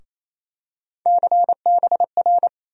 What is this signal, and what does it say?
Morse code, heard as a keyed tone, about 28 wpm, 700 Hz. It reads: CBR
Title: USA Call Sign Postfixes - 3 Letters 28wpm